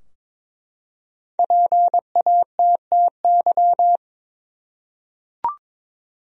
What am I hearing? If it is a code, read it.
Morse code, heard as a keyed tone, about 21 wpm, 700 Hz. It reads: PATTY
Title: USA Common Names - Max 5 Letters - 22wpm